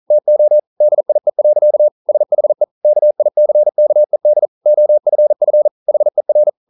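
Morse code, heard as a keyed tone, about 41 wpm, 600 Hz. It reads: TO DIE. SHE KICKED OFF HER
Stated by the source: Morse code practice at 41 wpm, from CW med Gnister.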